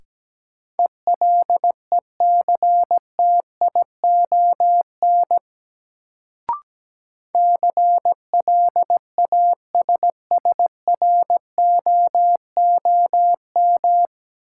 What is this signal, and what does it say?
Morse code, heard as a keyed tone, about 17 wpm, 700 Hz. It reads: ELECTION CLASSROOM
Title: Sentences from Top 1500 Words - New Word Drill Alt - 17wpm